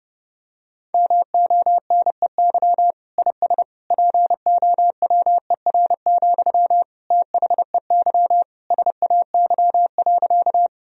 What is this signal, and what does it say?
Morse code, heard as a keyed tone, about 30 wpm, 700 Hz. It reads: MONEY IS POWER, THEY SAY.
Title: Sentences from Top 1000 Words - Review Part 2 - 30wpm